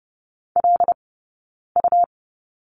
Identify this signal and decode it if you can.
Morse code, heard as a keyed tone, about 30 wpm, 700 Hz. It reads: L U
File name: Lesson 20d - Number 3 - Two Characters 30wpm